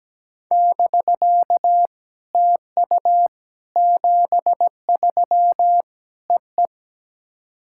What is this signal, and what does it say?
Morse code, heard as a keyed tone, about 17 wpm, 700 Hz. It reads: <BK> TU 73 E E
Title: SOTA QSO Practice 17wpm